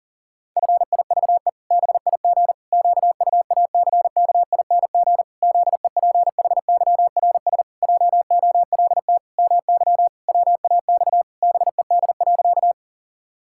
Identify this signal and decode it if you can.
Morse code, heard as a keyed tone, about 40 wpm, 700 Hz. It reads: FIVE BIG QUACKING ZEPHYRS JOLT MY WAX BED.
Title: Pangram Challenge v2 - 40wpm